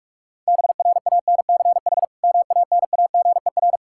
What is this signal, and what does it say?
Morse code, heard as a keyed tone, about 47 wpm, 700 Hz. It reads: BRANCH MANAGER
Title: Practice QSO - Exchange 3 - More Info - 45wpm